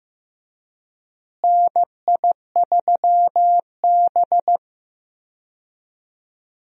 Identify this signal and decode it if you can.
Morse code, heard as a keyed tone, about 15 wpm, 700 Hz. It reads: NI3B
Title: USA Call Signs - 4 Characters - Repeated 2x - 15wpm